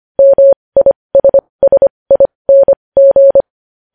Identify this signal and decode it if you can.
Morse code, heard as a keyed tone, about 25 wpm, spaced as for 20 wpm, 550 Hz. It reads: MISSING
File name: podcast / News headlines-1x-25x20